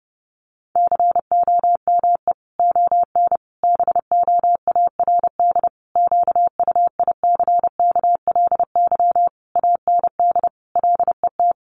Morse code, heard as a keyed tone, about 30 wpm, 700 Hz. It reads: COME ON BOARD QUICKLY AND LET
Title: Sentences from Top 1300 Words 30wpm